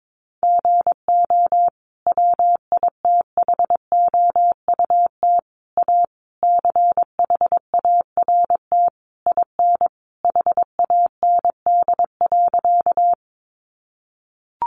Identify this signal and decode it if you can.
Morse code, heard as a keyed tone, about 22 wpm, 700 Hz. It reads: GO WITHOUT A CHART IN HAND.
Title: Sentences from Top 1300 Words 22wpm